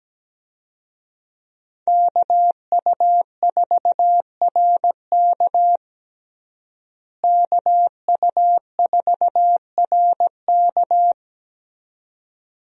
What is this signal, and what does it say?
Morse code, heard as a keyed tone, about 17 wpm, 700 Hz. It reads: KU4RK KU4RK
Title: USA Call Signs - 5 Characters - Repeated 2x - 17wpm